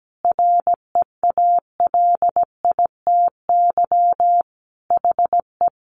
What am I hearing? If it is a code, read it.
Morse code, heard as a keyed tone, about 17 wpm, 700 Hz. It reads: REALITY HE
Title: Sentences from Top 1200 Words 17wpm